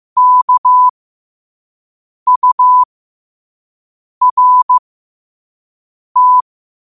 Morse code, heard as a keyed tone, about 15 wpm, spaced as for 5 wpm, 1000 Hz. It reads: KURT
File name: qotdfiles / 5wpm